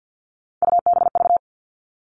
Morse code, heard as a keyed tone, about 50 wpm, 700 Hz. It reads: UBV
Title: All Cut Numbers - 3-Digit 50wpm